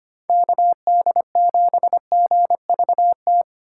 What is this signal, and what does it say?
Morse code, heard as a keyed tone, about 25 wpm, 700 Hz. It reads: KD7GVT
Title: USA Call Signs - 4x Character Spacing - 25wpm